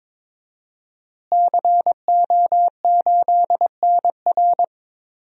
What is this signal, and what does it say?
Morse code, heard as a keyed tone, about 22 wpm, 700 Hz. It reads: CO8NR
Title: DXCC-100 Call Signs - 5 Characters - Repeated 3x - 22wpm